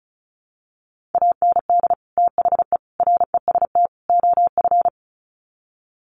Morse code, heard as a keyed tone, about 34 wpm, 700 Hz. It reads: AND THE REST OF
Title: Sets of 4 Words 35wpm